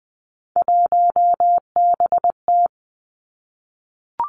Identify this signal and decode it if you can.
Morse code, heard as a keyed tone, about 20 wpm, 700 Hz. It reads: 1BT
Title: Three Challenging Letter-Number Combinations 20wpm